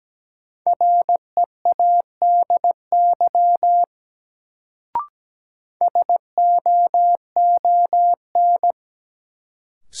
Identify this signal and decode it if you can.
Morse code, heard as a keyed tone, about 17 wpm, 700 Hz. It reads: READY SOON
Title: Sentences from Top 400 Words - New Word Drill - 17wpm